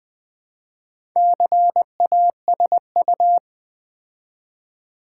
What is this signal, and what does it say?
Morse code, heard as a keyed tone, about 20 wpm, 700 Hz. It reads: CASU
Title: Build A Word - 2 Letters At A Time - 8-Letter Words - 20wpm